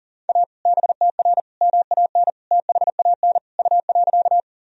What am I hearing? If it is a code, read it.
Morse code, heard as a keyed tone, about 40 wpm, 700 Hz. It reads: A BTR MAN THAN U.
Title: Sentences from Top 400 Words and Abbreviations 40wpm